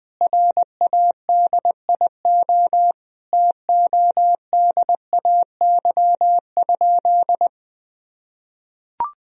Morse code, heard as a keyed tone, about 20 wpm, 700 Hz. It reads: RADIO TODAY?
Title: Sentences from Top 1000 Words - Review Part 3 - 20wpm